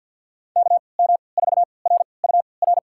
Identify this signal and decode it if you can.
Morse code, heard as a keyed tone, about 47 wpm, spaced as for 28 wpm, 700 Hz. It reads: KM4RUR
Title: USA Call Signs - 6 Characters - 3x Character Spacing - 50wpm